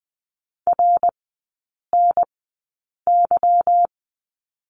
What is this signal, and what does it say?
Morse code, heard as a keyed tone, about 20 wpm, spaced as for 8 wpm, 700 Hz. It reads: RNY
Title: 159 Lesson 26e - Letter g - Three Characters 20wpm